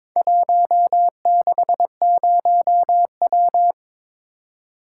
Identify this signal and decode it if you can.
Morse code, heard as a keyed tone, about 22 wpm, 690 Hz. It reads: 160W
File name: practice / 22wpm / Four Letter-Number Combinations - 22wpm